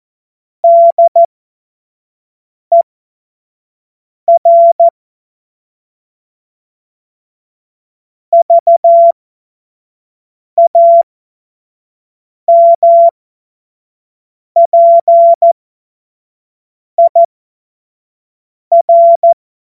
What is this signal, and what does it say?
Morse code, heard as a keyed tone, about 14 wpm, spaced as for 5 wpm, 680 Hz. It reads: DER VAMPIR